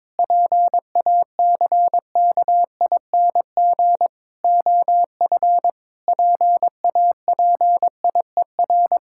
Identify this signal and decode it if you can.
Morse code, heard as a keyed tone, about 22 wpm, 700 Hz. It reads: PACKING OF PAPIER
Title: Pangram Challenge v2 - 22wpm